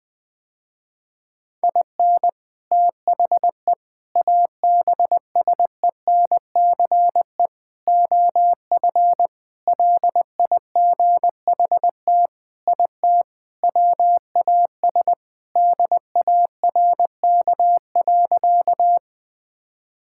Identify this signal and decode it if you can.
Morse code, heard as a keyed tone, about 20 wpm, 700 Hz. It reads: IN THE ABSENCE OF LIGHT IT WAS DARK.
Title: Sentences with 4-Word Phrases - Spoken Beginning - 20wpm